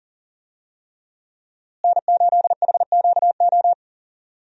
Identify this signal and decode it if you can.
Morse code, heard as a keyed tone, about 40 wpm, 700 Hz. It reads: N8HQO
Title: USA Call Signs - 5 Characters - Repeated 3x - 40wpm